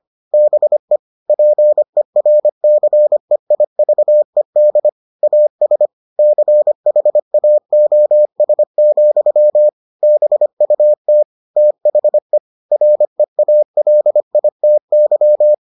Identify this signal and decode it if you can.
Morse code, heard as a keyed tone, about 25 wpm, 600 Hz. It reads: BE PERCEIVED AS CHAOS, BUT THE REALITY